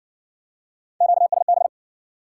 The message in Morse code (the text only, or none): QSB